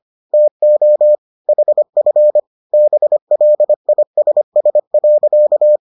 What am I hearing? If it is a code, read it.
Morse code, heard as a keyed tone, about 25 wpm, 600 Hz. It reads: TO HF BLISS.